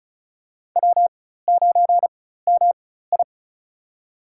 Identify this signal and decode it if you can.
Morse code, heard as a keyed tone, about 36 wpm, spaced as for 16 wpm, 700 Hz. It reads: W9MI
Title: USA Call Signs - 4 Characters - 4x Character Spacing - 35wpm